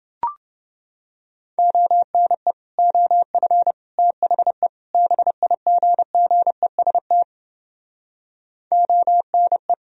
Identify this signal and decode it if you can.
Morse code, heard as a keyed tone, about 31 wpm, 700 Hz. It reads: ONE OF THE BIGGEST ONE